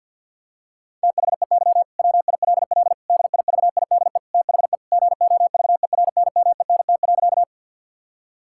Morse code, heard as a keyed tone, about 50 wpm, 700 Hz. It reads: THEY WILL DIVIDE THE GOVERNMENT.
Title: Sentences from Top 1600 Words 50wpm